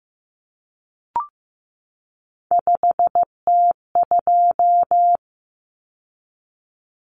Morse code, heard as a keyed tone, about 15 wpm, 700 Hz. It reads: 5T2
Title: Common Cut Numbers - 3-Digit 15wpm